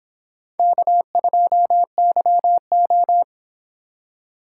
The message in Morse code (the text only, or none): K2YO